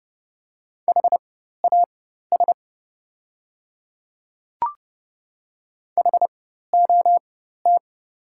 Morse code, heard as a keyed tone, about 30 wpm, spaced as for 14 wpm, 700 Hz. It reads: HAS HOT
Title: 4x Character Spacing - Top 200 Words - 30wpm